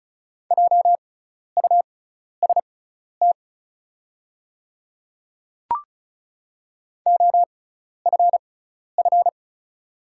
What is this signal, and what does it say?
Morse code, heard as a keyed tone, about 34 wpm, spaced as for 12 wpm, 690 Hz. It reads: JUST OFF
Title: 6x Character Spacing - Top 500 Words - 35wpm